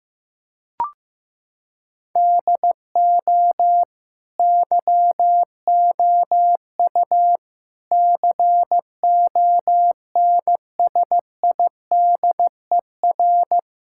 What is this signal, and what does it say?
Morse code, heard as a keyed tone, about 15 wpm, 700 Hz. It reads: DO YOU CONSIDER
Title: Sentences from Top 1000 Words - Review Part 6 - 15wpm